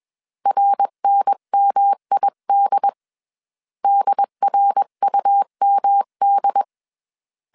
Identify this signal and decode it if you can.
Morse code, heard as a keyed tone, about 21 wpm, 800 Hz. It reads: RNMID DRUMD